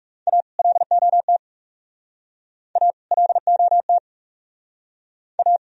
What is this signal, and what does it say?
Morse code, heard as a keyed tone, about 38 wpm, 700 Hz. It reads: A LOT A LOT A